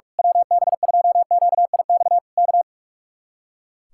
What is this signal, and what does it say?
Morse code, heard as a keyed tone, about 45 wpm, 700 Hz. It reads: WB2QIX K
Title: Speed-Racing Common QSO Phrases - 45wpm